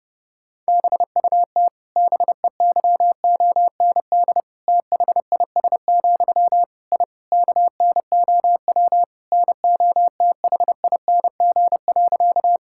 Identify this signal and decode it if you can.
Morse code, heard as a keyed tone, about 30 wpm, 700 Hz. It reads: BUT BEYOND THIS, I KNOW NOTHING.
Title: Sentences from Top 1000 Words - Review Part 6 - 30wpm